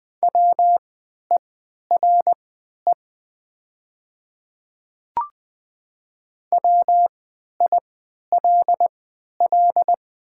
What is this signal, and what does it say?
Morse code, heard as a keyed tone, about 20 wpm, spaced as for 11 wpm, 700 Hz. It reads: WERE WILL
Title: 3x Character Spacing - Top 100 Words - 20wpm